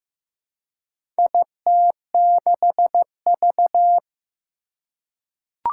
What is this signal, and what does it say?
Morse code, heard as a keyed tone, about 15 wpm, 700 Hz. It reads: IT6V
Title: Four Challenging Letter-Number Combinations 15wpm